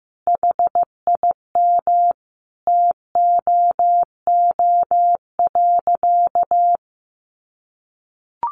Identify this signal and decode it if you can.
Morse code, heard as a keyed tone, about 15 wpm, 700 Hz. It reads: HIM TOO.
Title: Sentences from Top 1400 Words 15wpm